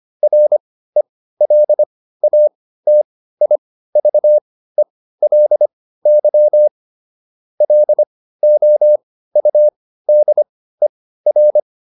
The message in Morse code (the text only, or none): RELATIVELY LOUDER